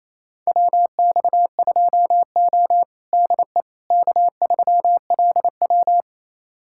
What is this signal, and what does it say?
Morse code, heard as a keyed tone, about 28 wpm, 700 Hz. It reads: WX2O DE K3LW
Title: USA Callsign DE Callsign - 4 Characters - 28wpm